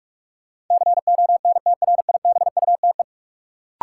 Morse code, heard as a keyed tone, about 43 wpm, 700 Hz. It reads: CONTRIBUTE
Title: Sentences from Top 1400 Words - New Word Drill - 45wpm